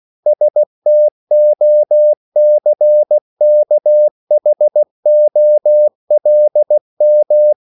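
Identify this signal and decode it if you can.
Morse code, heard as a keyed tone, about 16 wpm, 600 Hz. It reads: STOCKHOLM
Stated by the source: CW practice of national capitals at 16 words per minute, from ON6ZQ.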